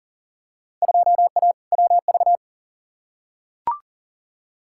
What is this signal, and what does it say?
Morse code, heard as a keyed tone, about 40 wpm, 700 Hz. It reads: 2A WV
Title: Field Day Exchanges Encore - 40wpm